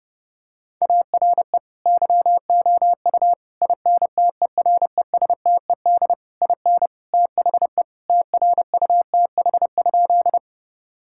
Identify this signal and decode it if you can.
Morse code, heard as a keyed tone, about 32 wpm, 700 Hz. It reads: ARE YOU INTERESTED IN THE TRUTH?
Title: Sentences from Top 1000 Words - Review Part 6 - 30wpm